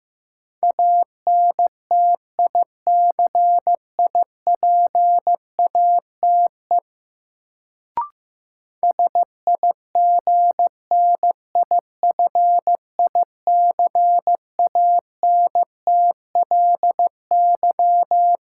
Morse code, heard as a keyed tone, about 15 wpm, 700 Hz. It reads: ANTICIPATE SIGNIFICANTLY
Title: Sentences from Top 1500 Words - New Word Drill - 15wpm